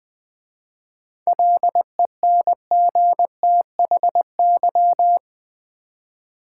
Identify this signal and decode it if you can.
Morse code, heard as a keyed tone, about 20 wpm, 690 Hz. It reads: LENGTHY
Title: 7 Letter Words - First 3 Letters Given - 20wpm